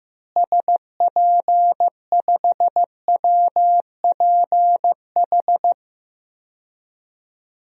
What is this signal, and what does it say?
Morse code, heard as a keyed tone, about 15 wpm, 700 Hz. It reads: SP5WPH
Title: DXCC-100 Call Signs - 6 Characters - Repeated 3x - 15wpm